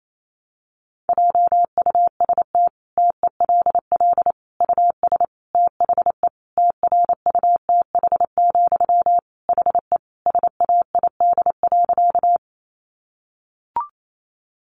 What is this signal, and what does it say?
Morse code, heard as a keyed tone, about 28 wpm, 700 Hz. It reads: JUST TELL US THE TRUTH, HE SAID.
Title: Sentences from Top 800 Words 28wpm